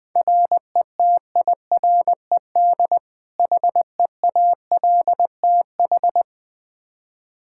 Morse code, heard as a keyed tone, about 21 wpm, 700 Hz. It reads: RETIRED HEALTH
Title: Practice QSO - Exchange 3 - More Info - 20wpm